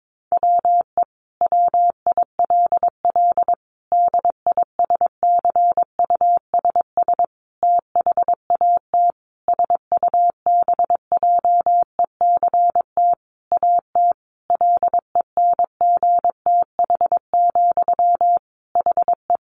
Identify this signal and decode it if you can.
Morse code, heard as a keyed tone, about 22 wpm, 700 Hz. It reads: WE WILL DISCUSS THAT SUBJECT AT LENGTH, HE
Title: Sentences from Top 1000 Words 22wpm